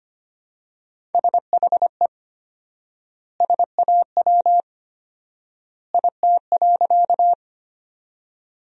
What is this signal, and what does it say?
Morse code, heard as a keyed tone, about 25 wpm, 700 Hz. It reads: SHE SAW IT.
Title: 4x Word-Spacing - Sentences from Top 300 Words - 25wpm